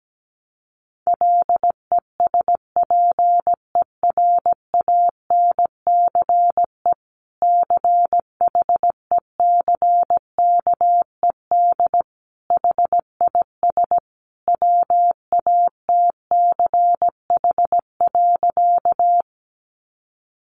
Morse code, heard as a keyed tone, about 17 wpm, 700 Hz. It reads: LESPERANCE CHECKED HIS WATCH.
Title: A Sound Of Thunder 17wpm